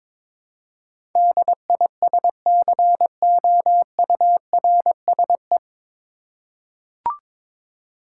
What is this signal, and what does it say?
Morse code, heard as a keyed tone, about 22 wpm, 700 Hz. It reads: DISCOURSE
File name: 9 Letter Words - First 3 Letters Given - 22wpm